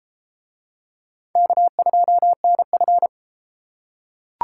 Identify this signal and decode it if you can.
Morse code, heard as a keyed tone, about 33 wpm, 700 Hz. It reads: K2NF